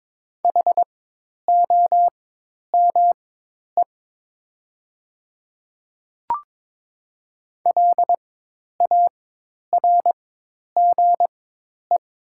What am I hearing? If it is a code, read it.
Morse code, heard as a keyed tone, about 22 wpm, spaced as for 10 wpm, 700 Hz. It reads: HOME LARGE
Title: 4x Character Spacing - Top 500 Words - 22wpm